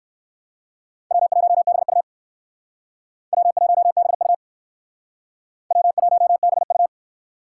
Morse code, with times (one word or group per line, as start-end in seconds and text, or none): 1.1-2.0 W1BU
3.3-4.3 W1BU
5.7-6.9 W1BU